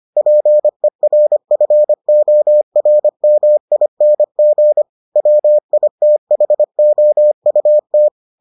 Morse code, heard as a keyed tone, about 25 wpm, 600 Hz. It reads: PERFORMING WITHOUT